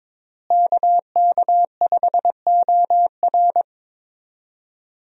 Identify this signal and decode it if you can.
Morse code, heard as a keyed tone, about 22 wpm, 700 Hz. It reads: KK5OR
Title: USA Call Signs - 5 Characters - Repeated 3x - 22wpm